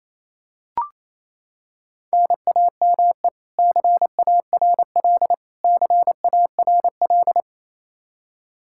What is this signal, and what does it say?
Morse code, heard as a keyed tone, about 28 wpm, 700 Hz. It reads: NAME CARL CARL
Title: K1USN SST QSO Practice 28wpm